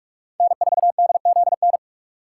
Morse code, heard as a keyed tone, about 44 wpm, 700 Hz. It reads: N4DZN